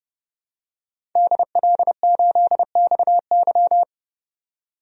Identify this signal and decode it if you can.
Morse code, heard as a keyed tone, about 32 wpm, 700 Hz. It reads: DL8XY
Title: DXCC-100 Call Signs - 5 Characters - 30wpm